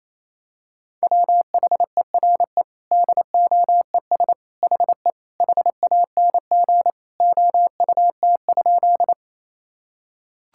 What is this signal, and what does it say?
Morse code, heard as a keyed tone, about 28 wpm, 700 Hz. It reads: WHERE DOES HE HANG OUT?
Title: Sentences from Top 1600 Words 28wpm